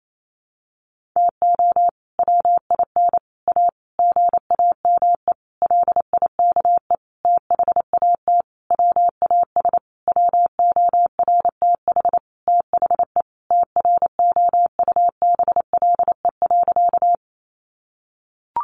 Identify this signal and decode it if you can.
Morse code, heard as a keyed tone, about 28 wpm, 700 Hz. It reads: TO WIN A GAME LIKE THAT WAS WORTH THE TROUBLE.
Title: Sentences from Top 1500 Words 28wpm